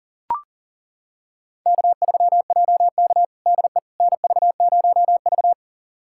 Morse code, heard as a keyed tone, about 40 wpm, 700 Hz. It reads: K3JK DE NV0V
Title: USA Callsign DE Callsign - 4 Characters - 40wpm